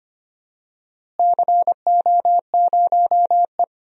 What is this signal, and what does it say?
Morse code, heard as a keyed tone, about 25 wpm, 700 Hz. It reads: CO0E
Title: DXCC-100 Call Signs - 4 Characters - Repeated 3x - 25wpm